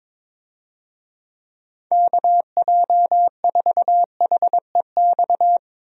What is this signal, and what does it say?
Morse code, heard as a keyed tone, about 21 wpm, 700 Hz. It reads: KJ4HEX
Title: USA Call Signs - 6 Characters - Repeated 2x - 22wpm